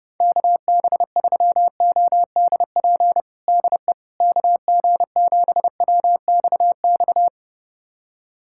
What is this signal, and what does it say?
Morse code, heard as a keyed tone, about 29 wpm, 700 Hz. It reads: KB3ODP DE KG7WXX
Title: USA Callsign DE Callsign 30wpm